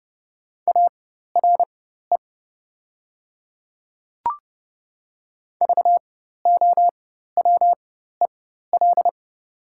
Morse code, heard as a keyed tone, about 29 wpm, spaced as for 14 wpm, 690 Hz. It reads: ARE VOWEL
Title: 4x Character Spacing - Top 500 Words - 30wpm